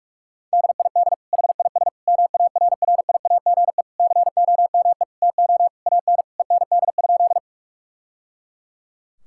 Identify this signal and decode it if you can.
Morse code, heard as a keyed tone, about 41 wpm, 700 Hz. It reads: DID HIS MARRIAGE COME TO AN END?
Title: Sentences from Top 1000 Words - Review Part 5 - 45wpm